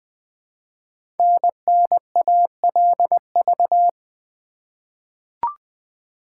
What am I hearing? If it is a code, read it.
Morse code, heard as a keyed tone, about 20 wpm, 700 Hz. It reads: NNALV
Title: Code Groups - 5 Letter:Numbers - 20wpm